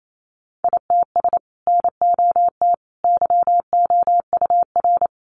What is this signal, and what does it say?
Morse code, heard as a keyed tone, about 28 wpm, 700 Hz. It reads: ITS NOT YOUR
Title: Sentences from Top 600 Words 28wpm